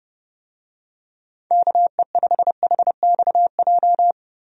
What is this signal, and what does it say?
Morse code, heard as a keyed tone, about 30 wpm, 700 Hz. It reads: KE5HXJ
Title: USA Call Signs - 6 Characters - Repeated 2x - 30wpm